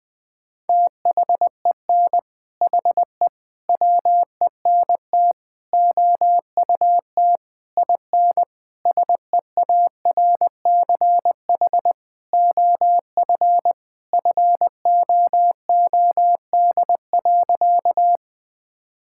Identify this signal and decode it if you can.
Morse code, heard as a keyed tone, about 20 wpm, 700 Hz. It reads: THEN HE WENT OUT IN SEARCH OF FOOD.
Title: Sentences from Top 1000 Words - Review Part 5 - 20wpm